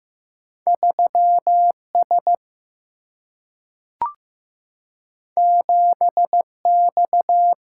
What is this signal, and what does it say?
Morse code, heard as a keyed tone, about 15 wpm, 700 Hz. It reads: 3 S 7 X
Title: Single-Letter Single-Number 15wpm